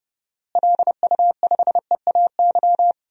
Morse code, heard as a keyed tone, about 32 wpm, 700 Hz. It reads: LU5EAY
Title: DXCC-100 Call Signs - 6 Characters - Repeated 3x - 30wpm